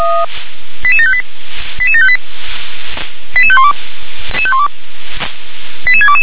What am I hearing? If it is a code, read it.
Five-tone selective call (ZVEI), heard as quick runs of five tones, tones of about 70 ms. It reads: 88758 88758 88512 88622
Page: /357/SDR/ZVEI/88730/